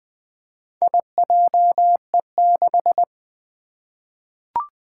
Morse code, heard as a keyed tone, about 20 wpm, 700 Hz. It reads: IJE6
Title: Four Challenging Letter-Number Combinations 20wpm